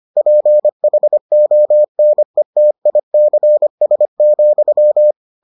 Morse code, heard as a keyed tone, about 25 wpm, 600 Hz. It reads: PHONETICS,